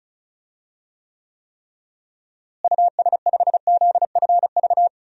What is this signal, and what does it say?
Morse code, heard as a keyed tone, about 36 wpm, 700 Hz. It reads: US5ZFV
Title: DXCC-100 Call Signs - 6 Characters - Repeated 3x - 35wpm